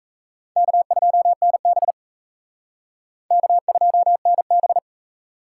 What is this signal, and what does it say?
Morse code, heard as a keyed tone, about 36 wpm, 700 Hz. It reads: K2NB K2NB